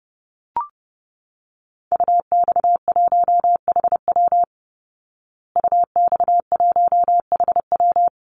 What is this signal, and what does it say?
Morse code, heard as a keyed tone, about 30 wpm, 700 Hz. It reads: UX1HW UX1HW
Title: Call Signs Repeated 3x - 30wpm